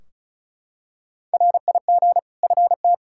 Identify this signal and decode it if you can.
Morse code, heard as a keyed tone, about 36 wpm, 700 Hz. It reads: RIG FT
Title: Practice QSO - Exchange 2 - Info - 35wpm